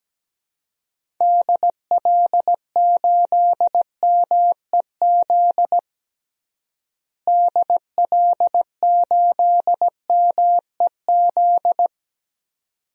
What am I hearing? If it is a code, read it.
Morse code, heard as a keyed tone, about 17 wpm, 700 Hz. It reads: DL8MEZ DL8MEZ
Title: DXCC-100 Call Signs - 6 Characters - Repeated 2x - 17wpm